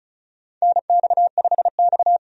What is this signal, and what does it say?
Morse code, heard as a keyed tone, about 34 wpm, 700 Hz. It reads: NX5X
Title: USA Call Signs - 4 Characters - 35wpm